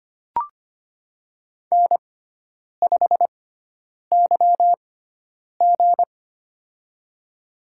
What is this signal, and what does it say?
Morse code, heard as a keyed tone, about 25 wpm, spaced as for 9 wpm, 700 Hz. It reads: N5YG
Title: USA Call Signs - 6x Character Spacing - 25wpm